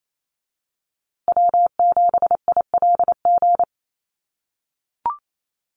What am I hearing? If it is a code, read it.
Morse code, heard as a keyed tone, about 28 wpm, 700 Hz. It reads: W7ILG
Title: USA Call Signs Repeated 3x 28wpm